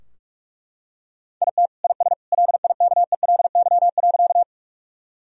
Morse code, heard as a keyed tone, about 41 wpm, 700 Hz. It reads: IT IS LIKELY.
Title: Sentences from Top 1000 Words - Review Part 4 - 45wpm